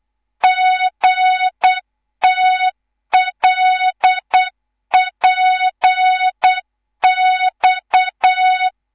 Morse code, heard as a keyed tone, about 8 wpm, 3700 Hz. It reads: GTLPX
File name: Morze-HA2MN / hang / 11 GYAKORLAT